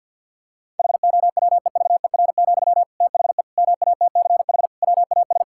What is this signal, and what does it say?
Morse code, heard as a keyed tone, about 50 wpm, 700 Hz. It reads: HOWEVER, THE MATCH WAS
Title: Sentences from Top 1600 Words 50wpm